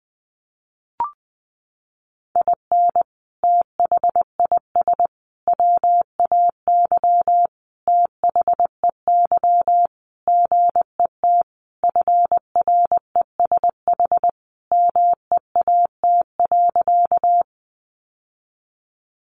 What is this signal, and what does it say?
Morse code, heard as a keyed tone, about 20 wpm, 700 Hz. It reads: IN THIS WAY THEY GET FRESH MEAT.
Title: Sentences from Top 1200 Words 20wpm